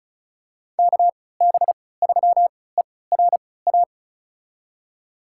Morse code, heard as a keyed tone, about 34 wpm, spaced as for 20 wpm, 700 Hz. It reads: KB3ERA
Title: USA Call Signs - 6 Characters - 3x Character Spacing - 35wpm